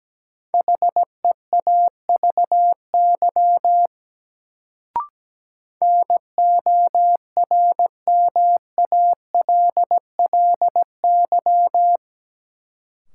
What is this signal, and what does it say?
Morse code, heard as a keyed tone, about 17 wpm, 700 Hz. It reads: HEAVY NORMALLY
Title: Sentences from Top 1000 Words - Review Part 6 - New Word Drill - 17wpm